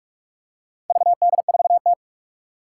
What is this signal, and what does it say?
Morse code, heard as a keyed tone, about 46 wpm, 700 Hz. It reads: VD4T
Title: Four Challenging Letter-Number Combinations 45wpm